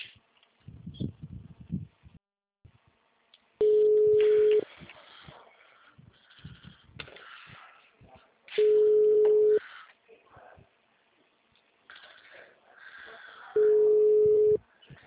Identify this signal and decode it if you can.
Ringback tone (Europe, 425 Hz), with speech.
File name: out- -2007- - - .67